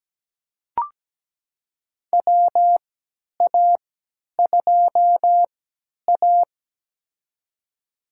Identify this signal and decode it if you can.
Morse code, heard as a keyed tone, about 17 wpm, spaced as for 10 wpm, 700 Hz. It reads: WA2A
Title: USA Call Signs - 4 Characters - 3x Character Spacing - 17wpm